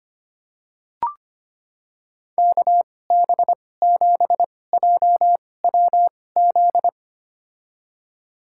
Morse code, heard as a keyed tone, about 25 wpm, spaced as for 18 wpm, 700 Hz. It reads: KB7JWZ
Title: USA Call Signs - 6 Characters - 2x Character Spacing - 25wpm